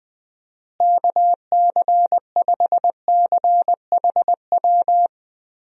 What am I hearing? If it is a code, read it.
Morse code, heard as a keyed tone, about 21 wpm, 700 Hz. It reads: KC5CHW
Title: USA Call Signs - 6 Characters - Repeated 3x - 20wpm